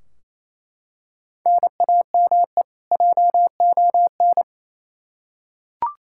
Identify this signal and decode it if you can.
Morse code, heard as a keyed tone, about 28 wpm, 700 Hz. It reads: NAME JON
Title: K1USN SST QSO Practice 28wpm